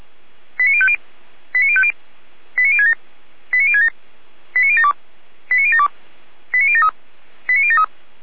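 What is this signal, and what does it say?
Five-tone selective call (ZVEI), heard as quick runs of five tones, tones of about 70 ms. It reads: 89050 89050 89067 89067 89072 89072 89073 89073
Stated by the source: ZVEI five-tone call 89760 received on SDR.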